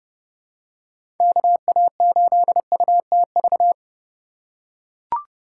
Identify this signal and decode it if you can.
Morse code, heard as a keyed tone, about 31 wpm, 700 Hz. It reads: KA8UTV
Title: USA Call Signs - 6 Characters - 2x Character Spacing - 30wpm